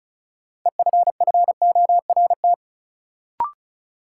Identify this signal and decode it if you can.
Morse code, heard as a keyed tone, about 36 wpm, 700 Hz. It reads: EFFORT
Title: Sets of 4 Words - New Word Drill - 35wpm